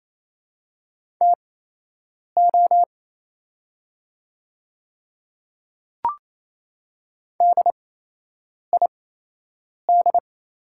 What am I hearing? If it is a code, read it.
Morse code, heard as a keyed tone, about 28 wpm, spaced as for 8 wpm, 690 Hz. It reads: TO DID
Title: 8x Character Spacing - Top 100 Words - 28wpm